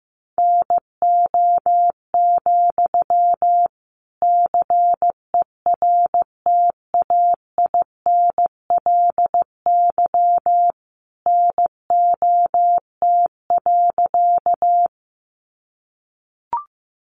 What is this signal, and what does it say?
Morse code, heard as a keyed tone, about 15 wpm, 700 Hz. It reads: NO, CERTAINLY NOT.
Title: Sentences from Top 900 Words 15wpm